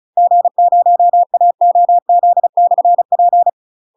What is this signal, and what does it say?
Morse code, heard as a keyed tone, about 36 wpm, 700 Hz. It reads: G0AOZ/P
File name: UK MASTER QSOs TRAINING / 35WPM / UK MASTER QSOs 0038